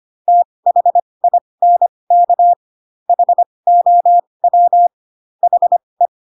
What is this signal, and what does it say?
Morse code, heard as a keyed tone, about 25 wpm, spaced as for 20 wpm, 700 Hz. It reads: THINK HOW HE